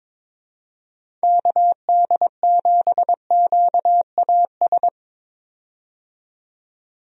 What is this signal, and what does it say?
Morse code, heard as a keyed tone, about 22 wpm, 700 Hz. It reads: KD7QAS
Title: USA Call Signs - 6 Characters - 22wpm